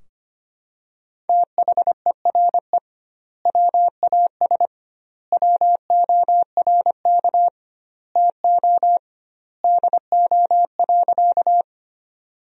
Morse code, heard as a keyed tone, about 25 wpm, 700 Hz. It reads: THERE WAS WORK TO DO.
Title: 2x Word-Spacing - Sentences from Top 200 Words - 25wpm